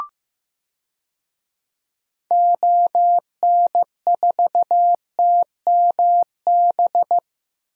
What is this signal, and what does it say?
Morse code, heard as a keyed tone, about 15 wpm, 700 Hz. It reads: ON4TMB
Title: DXCC-100 Call Signs - 6 Characters - Repeated 2x - 15wpm